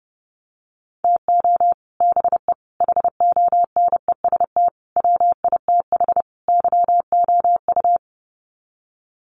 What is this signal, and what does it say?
Morse code, heard as a keyed tone, about 30 wpm, 700 Hz. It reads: TO BE HONEST WITH YOU
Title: Sets of 5 Words Encore 30wpm